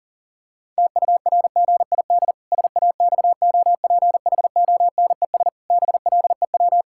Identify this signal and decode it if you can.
Morse code, heard as a keyed tone, about 40 wpm, 700 Hz. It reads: TURGID SAXOPHONES BLEW
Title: Pangram Challenge v2 - 40wpm